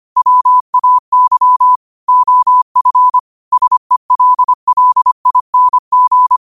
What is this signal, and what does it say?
Morse code, heard as a keyed tone, about 25 wpm, 1000 Hz. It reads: WAY OF SELLING
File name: qotdfiles / 25wpm